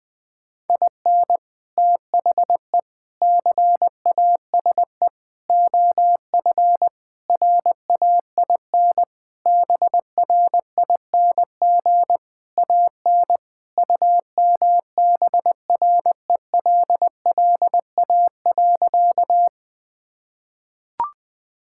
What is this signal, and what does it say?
Morse code, heard as a keyed tone, about 20 wpm, 700 Hz. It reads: IN THE CASE OF RAIN BRING AN UMBRELLA.
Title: Sentences with 3-Word Phrases - Spoken Beginning - 20wpm